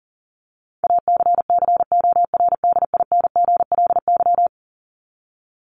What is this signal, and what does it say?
Morse code, heard as a keyed tone, about 40 wpm, 700 Hz. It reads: ACCORDINGLY
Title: LY Suffix - Top 100 Words with Base Words 40wpm